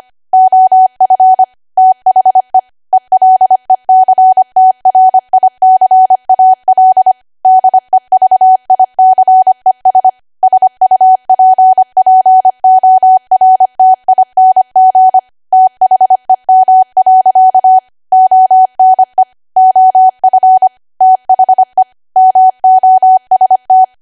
Morse code, heard as a keyed tone, about 25 wpm, 750 Hz. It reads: OF THE ELECTRICAL DEVICES SUPPORTING THEM. ONE OF THE MOST
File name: Archive / 25 WPM / 25WPM